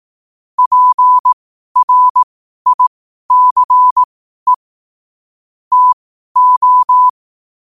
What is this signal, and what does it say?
Morse code, heard as a keyed tone, about 18 wpm, spaced as for 12 wpm, 1000 Hz. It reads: PRICE TO
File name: qotdfiles / 12wpm